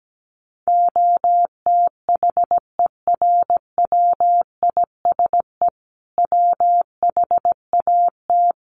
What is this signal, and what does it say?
Morse code, heard as a keyed tone, about 17 wpm, 700 Hz. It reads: OTHERWISE WHAT
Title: Sentences from Top 1200 Words 17wpm